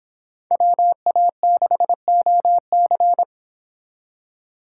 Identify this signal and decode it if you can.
Morse code, heard as a keyed tone, about 28 wpm, 700 Hz. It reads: WA6OC